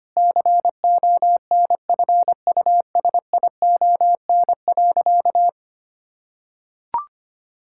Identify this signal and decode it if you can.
Morse code, heard as a keyed tone, about 25 wpm, 700 Hz. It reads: CONFUSION.
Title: Sentences from Top 1600 Words 25wpm